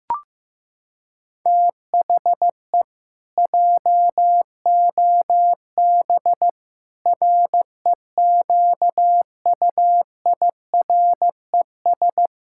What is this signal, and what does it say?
Morse code, heard as a keyed tone, about 15 wpm, 700 Hz. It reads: THE JOB REQUIRES
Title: Pangram Challenge v2 - 15wpm